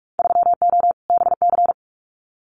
Morse code, heard as a keyed tone, about 46 wpm, 700 Hz. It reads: 3O BC
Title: Winter Field Day Exchanges Encore 45wpm